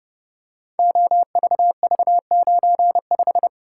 Morse code, heard as a keyed tone, about 30 wpm, 700 Hz. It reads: OVV95
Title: Code Groups - 5 Letter:Numbers - 30wpm